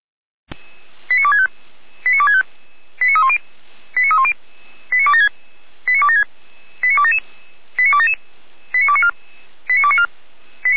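Five-tone selective call (ZVEI), heard as quick runs of five tones, tones of about 80 ms. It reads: 89265 89265 89310 89310 89276 89276 89277 89294 89294
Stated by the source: Signal received via SDR, ZVEI five-tone.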